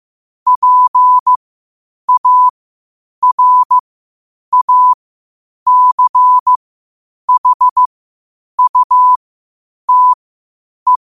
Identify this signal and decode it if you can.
Morse code, heard as a keyed tone, about 15 wpm, spaced as for 8 wpm, 1000 Hz. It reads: PARACHUTE
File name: qotdfiles / 8wpm